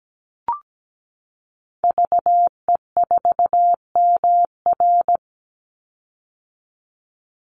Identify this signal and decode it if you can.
Morse code, heard as a keyed tone, about 17 wpm, 700 Hz. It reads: VE4MR
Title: Call Signs - 17wpm